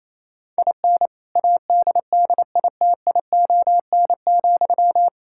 Morse code, heard as a keyed tone, about 28 wpm, 700 Hz. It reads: IN ADDITION,
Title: Sentences from Top 1000 Words - Review Part 6 - 28wpm